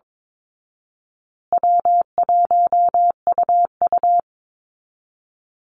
Morse code, heard as a keyed tone, about 22 wpm, 700 Hz. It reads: W1UU
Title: Easy Call Signs Repeated 2x - 22wpm